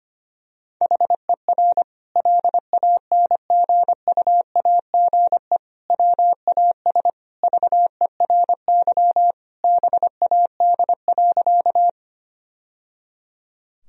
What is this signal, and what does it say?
Morse code, heard as a keyed tone, about 25 wpm, 700 Hz. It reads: HER LANGUAGE WAS VERY BAD.
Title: Sentences from Top 1000 Words - Review Part 3 - 25wpm